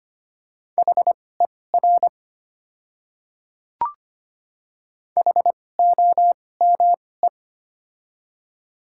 Morse code, heard as a keyed tone, about 25 wpm, spaced as for 18 wpm, 700 Hz. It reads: HER HOME
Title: 2x Character Spacing - Top 200 Words - 25wpm